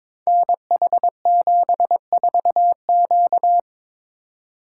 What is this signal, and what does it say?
Morse code, heard as a keyed tone, about 22 wpm, 700 Hz. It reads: NH74Q
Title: Code Groups - 5 Letter:Numbers - 22wpm